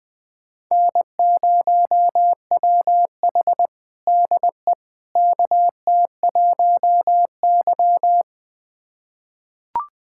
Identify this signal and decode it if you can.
Morse code, heard as a keyed tone, about 20 wpm, 700 Hz. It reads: N0WH DE KT1Y
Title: USA Callsign DE Callsign - 4 Characters - 20wpm